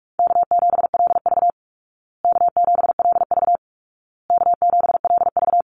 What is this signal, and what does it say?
Morse code, heard as a keyed tone, about 44 wpm, 700 Hz. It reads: K7LV K7LV K7LV
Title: Call Signs Repeated 2x - 45wpm